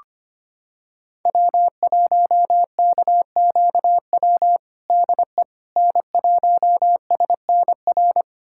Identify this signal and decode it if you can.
Morse code, heard as a keyed tone, about 25 wpm, 700 Hz. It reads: W1KQW DE N1SNR
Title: USA Callsign DE Callsign - 5 Characters - 25wpm